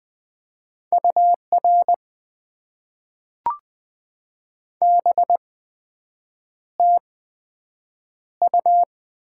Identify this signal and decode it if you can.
Morse code, heard as a keyed tone, about 20 wpm, spaced as for 7 wpm, 700 Hz. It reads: UR BTU
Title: Top 90 QSO Elements - 8x Character Spacing - 20wpm